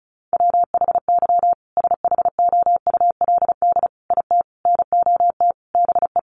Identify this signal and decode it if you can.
Morse code, heard as a keyed tone, about 34 wpm, 700 Hz. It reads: WHY SHOULD IT NOT BE
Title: Sentences from Top 1400 Words 35wpm